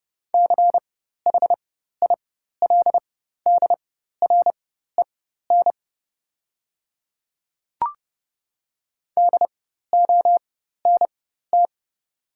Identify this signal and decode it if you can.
Morse code, heard as a keyed tone, about 30 wpm, spaced as for 14 wpm, 690 Hz. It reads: CHILDREN DONT
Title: 4x Character Spacing - Top 300 Words - 30wpm